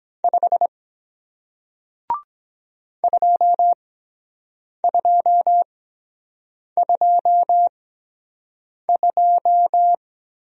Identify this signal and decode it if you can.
Morse code, heard as a keyed tone, about 23 wpm, spaced as for 7 wpm, 700 Hz. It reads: 5 2222